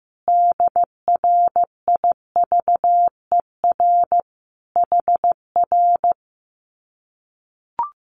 Morse code, heard as a keyed tone, about 14 wpm, 700 Hz. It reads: DRIVER HR
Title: Common QSO Phrases Encore 15wpm